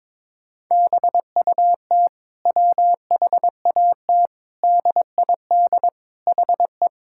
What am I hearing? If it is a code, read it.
Morse code, heard as a keyed tone, about 22 wpm, 700 Hz. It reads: BUT WHAT DID HE
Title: Sentences from Top 1000 Words - Review Part 6 - 22wpm